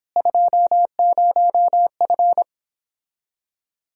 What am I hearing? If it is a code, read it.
Morse code, heard as a keyed tone, about 26 wpm, 700 Hz. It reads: 20F